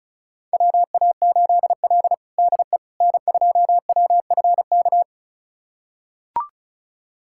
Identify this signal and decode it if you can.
Morse code, heard as a keyed tone, about 34 wpm, 700 Hz. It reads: WA8L DE N2WFK
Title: USA Callsign DE Callsign 35wpm